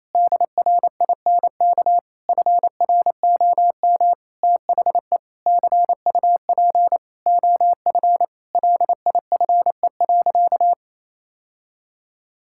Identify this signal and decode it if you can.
Morse code, heard as a keyed tone, about 28 wpm, 700 Hz. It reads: DRINK FROM THE CUP OF LIFE.
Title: Sentences from Top 1600 Words 28wpm